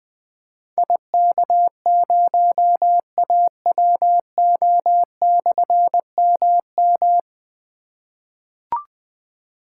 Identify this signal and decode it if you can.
Morse code, heard as a keyed tone, about 20 wpm, 700 Hz. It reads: IK0AWO/MM
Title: Difficult Call Signs 20wpm